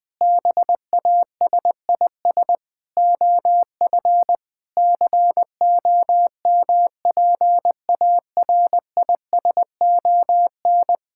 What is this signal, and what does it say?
Morse code, heard as a keyed tone, about 19 wpm, 700 Hz. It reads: BASIS OF COMPARISON
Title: Sentences from Top 1600 Words 20wpm